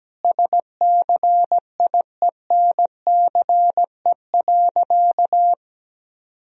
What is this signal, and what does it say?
Morse code, heard as a keyed tone, about 17 wpm, 700 Hz. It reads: SCIENCE.
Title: Sentences from Top 1000 Words - Review Part 2 - 17wpm